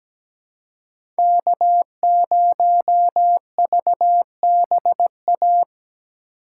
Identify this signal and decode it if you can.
Morse code, heard as a keyed tone, about 17 wpm, 700 Hz. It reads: K0VBA
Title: USA Call Signs - 5 Characters - 3x Character Spacing - 17wpm